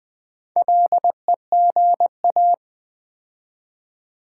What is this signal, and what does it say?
Morse code, heard as a keyed tone, about 20 wpm, 700 Hz. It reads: LEGA
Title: Build A Word - 2 Letters At A Time - 6-Letter Words - 20wpm